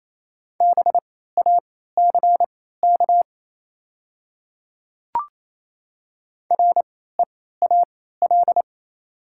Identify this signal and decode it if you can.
Morse code, heard as a keyed tone, about 27 wpm, spaced as for 16 wpm, 690 Hz. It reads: BACK REAL
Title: 3x Character Spacing - Top 300 Words - 28wpm